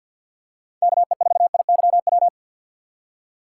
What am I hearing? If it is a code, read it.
Morse code, heard as a keyed tone, about 48 wpm, 690 Hz. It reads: KE4IYW
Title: USA Call Signs - 2x Character Spacing - 50wpm